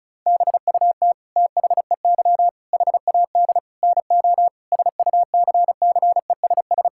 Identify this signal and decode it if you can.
Morse code, heard as a keyed tone, about 36 wpm, 700 Hz. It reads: BUT THEY HAD NO SUCCESS
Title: Sentences from Top 1000 Words - Review Part 6 - 35wpm